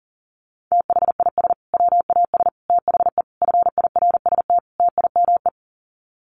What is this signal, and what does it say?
Morse code, heard as a keyed tone, about 40 wpm, 700 Hz. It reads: THIS WAS THE FIRST TIME
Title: Sets of 5 Words Encore 40wpm